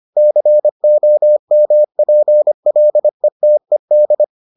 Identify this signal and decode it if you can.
Morse code, heard as a keyed tone, about 25 wpm, 600 Hz. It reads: COMPLETED